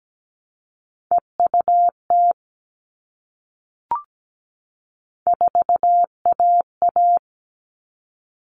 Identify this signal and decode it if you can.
Morse code, heard as a keyed tone, about 17 wpm, 700 Hz. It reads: EUT 4AA
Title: All Cut Numbers - 3-Digit 17wpm